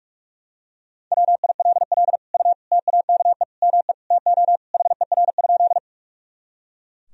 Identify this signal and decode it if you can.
Morse code, heard as a keyed tone, about 43 wpm, 700 Hz. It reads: WILL U TAKE ME TO HER?
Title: Sentences from Top 200 Words and Abbreviations 45wpm